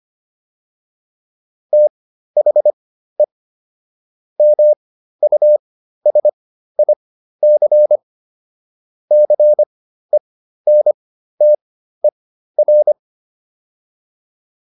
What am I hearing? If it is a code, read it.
Morse code, heard as a keyed tone, about 25 wpm, spaced as for 13 wpm, 600 Hz. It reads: THE MUSIC CENTER
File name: PR404 13